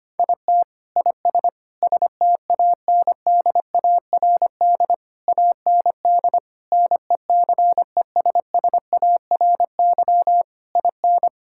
Morse code, heard as a keyed tone, about 25 wpm, 700 Hz. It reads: IT IS STANDARD AND NECESSARY IN
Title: Sentences from Top 1000 Words - Review Part 6 - 25wpm